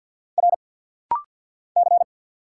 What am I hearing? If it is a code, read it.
Morse code, heard as a keyed tone, about 48 wpm, 690 Hz. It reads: R C
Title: Single Letters - Rapid-Fire 50wpm